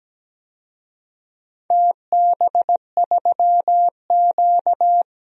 Morse code, heard as a keyed tone, about 17 wpm, 700 Hz. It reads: TB3Q
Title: DXCC-100 Call Signs - 4 Characters - Repeated 2x - 17wpm